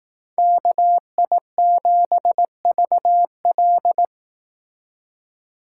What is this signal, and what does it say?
Morse code, heard as a keyed tone, about 18 wpm, 700 Hz. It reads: KI7VL
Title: Speed Racing - USA 2x2 Call Signs 15wpm